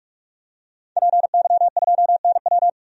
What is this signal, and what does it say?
Morse code, heard as a keyed tone, about 43 wpm, 700 Hz. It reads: PY2NW
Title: DXCC-100 Call Signs - 5 Characters - 45wpm